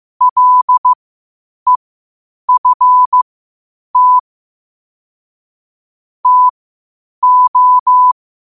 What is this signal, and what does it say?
Morse code, heard as a keyed tone, about 15 wpm, spaced as for 8 wpm, 1000 Hz. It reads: LEFT TO
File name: qotdfiles / 8wpm